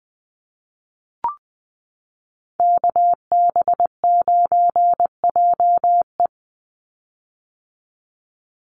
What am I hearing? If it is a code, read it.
Morse code, heard as a keyed tone, about 20 wpm, 700 Hz. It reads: KB9JE
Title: USA Call Signs 20wpm